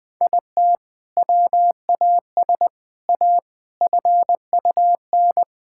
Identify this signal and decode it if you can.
Morse code, heard as a keyed tone, about 20 wpm, 700 Hz. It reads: IT WAS A FUN
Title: Sentences from Top 1000 Words - Review Part 5 - 20wpm